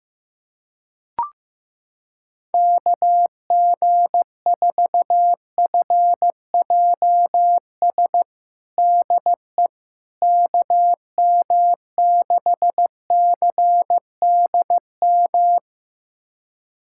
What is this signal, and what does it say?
Morse code, heard as a keyed tone, about 15 wpm, 700 Hz. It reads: KG4FJS DE KM6CDM
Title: USA Callsign DE Callsign 15wpm